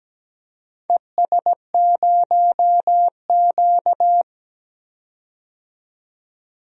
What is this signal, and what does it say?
Morse code, heard as a keyed tone, about 17 wpm, 700 Hz. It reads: ES0Q
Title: DXCC-100 Call Signs - 4 Characters - 17wpm